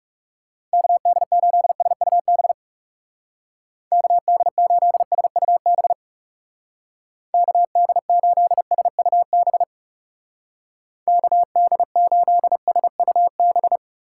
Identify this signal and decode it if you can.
Morse code, heard as a keyed tone, about 38 wpm, 700 Hz. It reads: KD8SUB KD8SUB KD8SUB KD8SUB